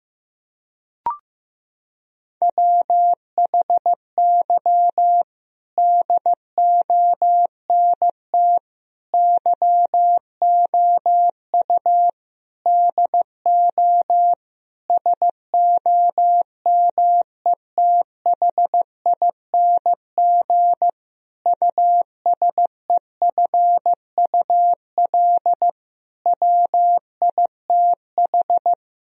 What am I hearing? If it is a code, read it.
Morse code, heard as a keyed tone, about 16 wpm, 700 Hz. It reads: WHY DONT YOU DO SOMETHING USEFUL WITH
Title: Sentences from Top 1000 Words - Review Part 4 - 15wpm